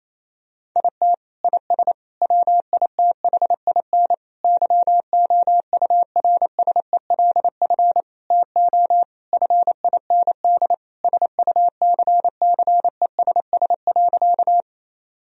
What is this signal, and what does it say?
Morse code, heard as a keyed tone, about 28 wpm, 700 Hz. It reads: IT IS WITHIN YOURSELF TO FIND SUCCESS.
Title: Sentences from Top 1000 Words - Review Part 4 - 28wpm